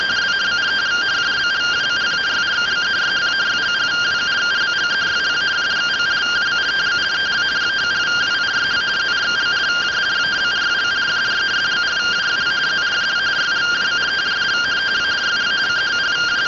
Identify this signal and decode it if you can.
Two-tone teleprinter signal, about 170 Hz shift.